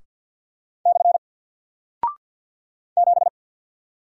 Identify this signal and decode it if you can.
Morse code, heard as a keyed tone, about 54 wpm, 700 Hz. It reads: / 7
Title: Single Character (Letter:Number:Punctuation:Prosign) 50wpm